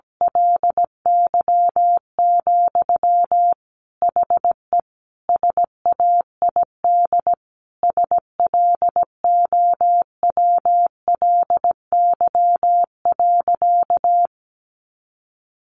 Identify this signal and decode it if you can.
Morse code, heard as a keyed tone, about 17 wpm, 700 Hz. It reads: LY, HE SAID SLOWLY.
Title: Sentences from Top 1100 Words 17wpm